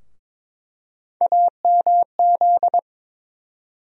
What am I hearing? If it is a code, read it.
Morse code, heard as a keyed tone, about 22 wpm, 700 Hz. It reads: AMZ
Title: USA Call Sign Postfixes - 3 Letters 22wpm